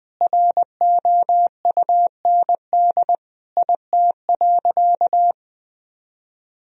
Morse code, heard as a keyed tone, about 20 wpm, 700 Hz. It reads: ROUND IT.
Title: Sentences from Top 1000 Words - Review Part 1 - 20wpm